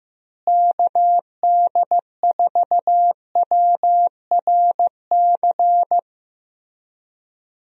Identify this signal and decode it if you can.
Morse code, heard as a keyed tone, about 15 wpm, 700 Hz. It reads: KD4WRC
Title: USA Call Signs - 6 Characters - Repeated 3x - 15wpm